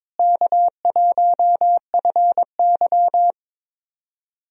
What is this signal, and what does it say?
Morse code, heard as a keyed tone, about 22 wpm, 700 Hz. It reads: K1FY
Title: USA Call Signs - 4 Characters - Repeated 2x - 22wpm